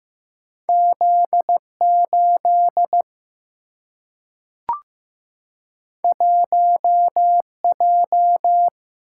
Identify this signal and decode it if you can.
Morse code, heard as a keyed tone, about 15 wpm, 700 Hz. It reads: Z8 1J
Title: Single-Letter Single-Number 15wpm